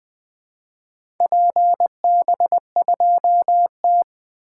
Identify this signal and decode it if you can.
Morse code, heard as a keyed tone, about 20 wpm, 700 Hz. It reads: PB2T
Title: 2023 SOTA Activator Call Signs - 20wpm